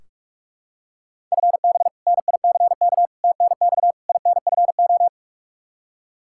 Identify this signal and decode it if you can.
Morse code, heard as a keyed tone, about 46 wpm, 700 Hz. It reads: FB NICK TNX INFO
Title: Practice QSO - Exchange 3 - More Info - 45wpm